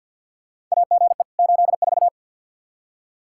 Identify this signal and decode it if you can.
Morse code, heard as a keyed tone, about 52 wpm, 700 Hz. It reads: AGE 84
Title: Practice QSO - Exchange 3 - More Info - 50wpm